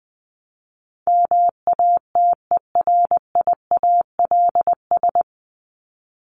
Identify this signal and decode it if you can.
Morse code, heard as a keyed tone, about 20 wpm, 700 Hz. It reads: MATERIALS
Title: ES:S Suffix - Top 100 Words with Base Words 20wpm